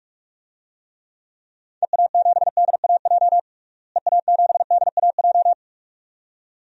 Morse code, heard as a keyed tone, about 41 wpm, 700 Hz. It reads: EA7DAJ EA7DAJ
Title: DXCC-100 Call Signs - 6 Characters - Repeated 2x - 45wpm